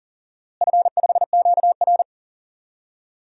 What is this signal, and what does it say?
Morse code, heard as a keyed tone, about 38 wpm, 700 Hz. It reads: F5QR
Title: DXCC-100 Call Signs - 4 Characters - Repeated 2x - 40wpm